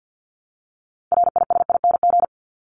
Morse code, heard as a keyed tone, about 50 wpm, 700 Hz. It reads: RISING
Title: ING Suffix - Top 100 Words with Base Words 50wpm